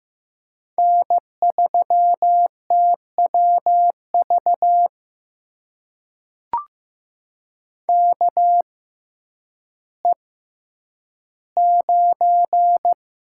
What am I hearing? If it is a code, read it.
Morse code, heard as a keyed tone, about 15 wpm, spaced as for 9 wpm, 700 Hz. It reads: N3TWV KE9
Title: USA Call Signs - 5 Characters - 6x Character Spacing - 15wpm